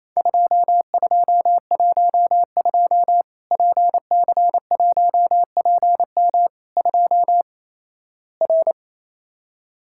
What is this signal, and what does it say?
Morse code, heard as a keyed tone, about 28 wpm, 700 Hz. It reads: 2212 PC1PM 2 R
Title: Worked All Europe DX Contest - QTC Exchange - 28wpm